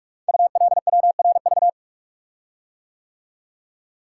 Morse code, heard as a keyed tone, about 44 wpm, 700 Hz. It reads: ULWRV
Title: Code Groups - 5 Letters - 45wpm